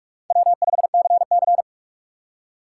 Morse code, heard as a keyed tone, about 46 wpm, 700 Hz. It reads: W5CC
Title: USA Call Signs - 6x Character Spacing - 45wpm